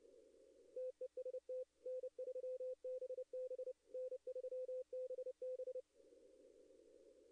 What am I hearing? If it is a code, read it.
Morse code, heard as a keyed tone, about 30 wpm, 510 Hz. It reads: TEST N3BB N3BB
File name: contests / arrl-160-k5tr-op-w5tn -2300